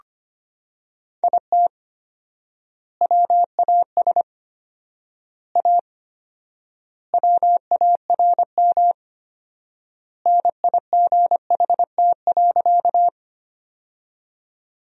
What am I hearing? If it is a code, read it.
Morse code, heard as a keyed tone, about 25 wpm, 700 Hz. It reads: IT WAS A WARM NIGHT.
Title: 4x Word-Spacing - Sentences from Top 500 Words - 25wpm